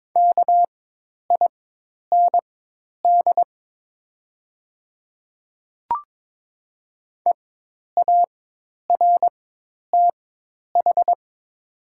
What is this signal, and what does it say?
Morse code, heard as a keyed tone, about 22 wpm, spaced as for 10 wpm, 700 Hz. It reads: KIND EARTH
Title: 4x Character Spacing - Top 500 Words - 22wpm